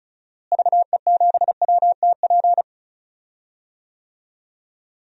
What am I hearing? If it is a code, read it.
Morse code, heard as a keyed tone, about 36 wpm, 700 Hz. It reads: VE7WTP
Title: DXCC-100 Call Signs - 6 Characters - Repeated 3x - 35wpm